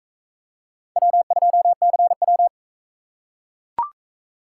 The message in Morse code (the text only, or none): W2CW